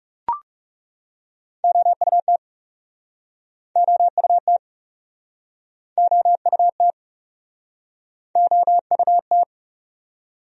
Morse code, heard as a keyed tone, about 44 wpm, 700 Hz. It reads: OUT OUT OUT OUT